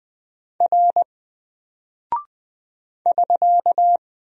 Morse code, heard as a keyed tone, about 20 wpm, 700 Hz. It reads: R <SK>
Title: Single Character (Letter:Number:Punctuation:Prosign) 20wpm